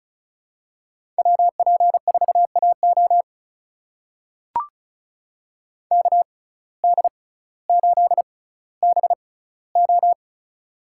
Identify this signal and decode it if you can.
Morse code, heard as a keyed tone, about 36 wpm, spaced as for 18 wpm, 700 Hz. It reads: WP4AO KD8BO
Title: USA Call Signs - 5 Characters - 6x Character Spacing - 35wpm